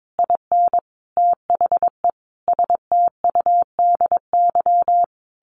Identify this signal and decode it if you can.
Morse code, heard as a keyed tone, about 22 wpm, 700 Hz. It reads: IN THE STUDY
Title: Sentences from Top 900 Words 22wpm